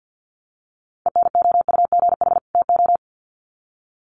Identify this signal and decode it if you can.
Morse code, heard as a keyed tone, about 50 wpm, 700 Hz. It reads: ENOUGH TO
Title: Sets of 2 Words 50wpm